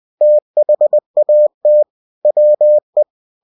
Morse code, heard as a keyed tone, about 20 wpm, 600 Hz. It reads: THAT WE
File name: cw-books / winnie-the-pooh / winnie-20wpm-page-0107